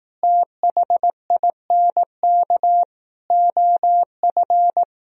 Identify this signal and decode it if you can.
Morse code, heard as a keyed tone, about 18 wpm, 700 Hz. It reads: THINK OF
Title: Speed-Racing - Sets of 5 Words 15wpm